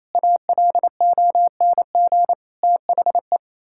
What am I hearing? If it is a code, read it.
Morse code, heard as a keyed tone, about 28 wpm, 700 Hz. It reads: ALONG THE
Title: Pangram Challenge v2 - 28wpm